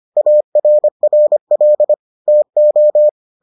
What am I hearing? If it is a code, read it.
Morse code, heard as a keyed tone, about 25 wpm, 600 Hz. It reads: ARRL TO